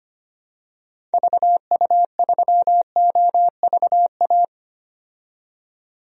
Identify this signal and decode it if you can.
Morse code, heard as a keyed tone, about 25 wpm, 700 Hz. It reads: VU3OVA
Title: DXCC-100 Call Signs - 6 Characters - 25wpm